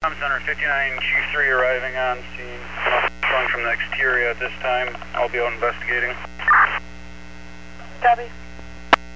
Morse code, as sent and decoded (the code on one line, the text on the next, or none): none